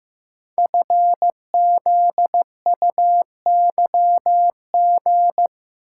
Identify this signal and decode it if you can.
Morse code, heard as a keyed tone, about 15 wpm, 700 Hz. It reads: FZUYG
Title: Code Groups - 5 Letter:Numbers - 15wpm